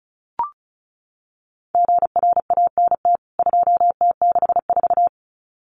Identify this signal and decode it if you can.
Morse code, heard as a keyed tone, about 34 wpm, 700 Hz. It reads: GRANT 2T64
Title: CWT Exchange 35wpm